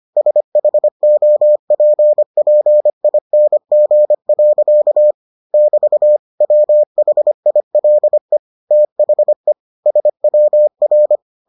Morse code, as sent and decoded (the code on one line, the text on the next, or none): ... .... --- .--. .--. .. -. --. .-.-.- / -...- / .-- .... .. .-.. / - .... / ... .-- .-.
SHOPPING. <BT> WHILE THE SWR